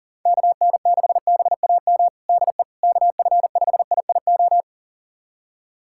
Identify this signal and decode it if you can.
Morse code, heard as a keyed tone, about 40 wpm, 700 Hz. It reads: KN6BAM DE KF5IIO
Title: USA Callsign DE Callsign - 6 Characters - 40wpm